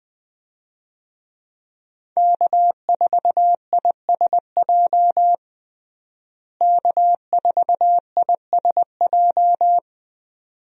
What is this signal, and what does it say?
Morse code, heard as a keyed tone, about 21 wpm, 700 Hz. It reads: K4ISJ K4ISJ
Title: USA Call Signs - 5 Characters - Repeated 2x - 20wpm